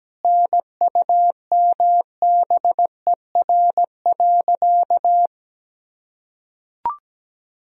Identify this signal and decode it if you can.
Morse code, heard as a keyed tone, about 17 wpm, 700 Hz. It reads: NUMBER.
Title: Sentences from Top 1000 Words - Review Part 1 - 17wpm